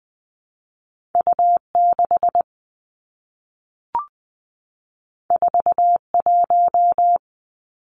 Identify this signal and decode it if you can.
Morse code, heard as a keyed tone, about 20 wpm, 700 Hz. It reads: U 6 4 1
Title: Two Challenging Letter-Number Combinations 20wpm